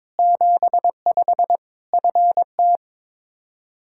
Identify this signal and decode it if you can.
Morse code, heard as a keyed tone, about 22 wpm, 700 Hz. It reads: 75 FT
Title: Practice QSO - Exchange 2 - Info - 22wpm